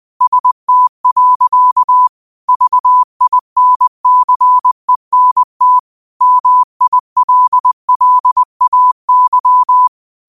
Morse code, heard as a keyed tone, about 20 wpm, 1000 Hz. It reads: ST. VINCENT MILLAY
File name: qotdfiles / 20wpm